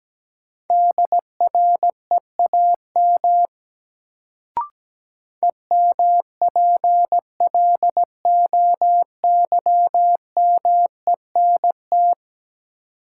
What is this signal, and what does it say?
Morse code, heard as a keyed tone, about 17 wpm, 700 Hz. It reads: DREAM EMPLOYMENT
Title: Sentences from Top 1500 Words - New Word Drill - 17wpm